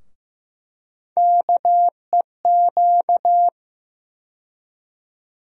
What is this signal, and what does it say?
Morse code, heard as a keyed tone, about 15 wpm, 700 Hz. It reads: KEQ
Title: USA Call Sign Postfixes - 3 Letters 15wpm